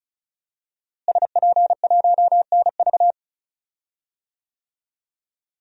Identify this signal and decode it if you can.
Morse code, heard as a keyed tone, about 36 wpm, 700 Hz. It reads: SP1NV
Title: DXCC-100 Call Signs - 5 Characters - Repeated 3x - 35wpm